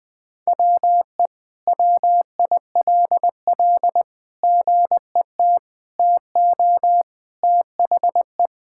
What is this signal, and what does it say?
Morse code, heard as a keyed tone, about 20 wpm, 700 Hz. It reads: WE WILL GET TO THE
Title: Sentences with 6-Word Phrases - 20wpm